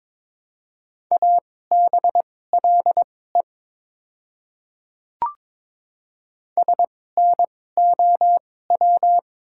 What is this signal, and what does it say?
Morse code, heard as a keyed tone, about 21 wpm, spaced as for 16 wpm, 690 Hz. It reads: ABLE SNOW
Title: 2x Character Spacing - Top 500 Words - 22wpm